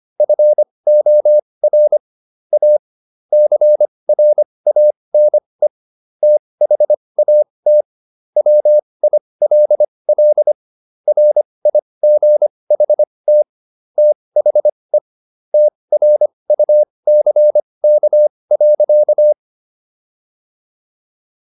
Morse code, heard as a keyed tone, about 25 wpm, spaced as for 20 wpm, 600 Hz. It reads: FOR A CRANE THAT WILL RIGHT THE TRUCK.